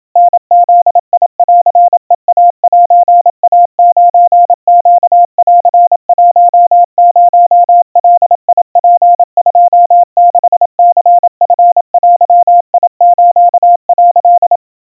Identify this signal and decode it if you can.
Morse code (Japanese Wabun code), heard as a keyed tone, about 27 wpm, 700 Hz. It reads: タブンヘイセイ9ネン10ガツ26ニチデス」
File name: kawa / Morse sample / kamyu 100 cpm 700 Hz Crispy